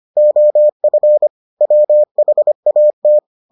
Morse code, heard as a keyed tone, about 25 wpm, 600 Hz. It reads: OF WHAT